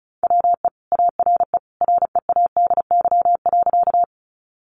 Morse code, heard as a keyed tone, about 34 wpm, 700 Hz. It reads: WE ARE READY.
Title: Sentences from Top 400 Words 35wpm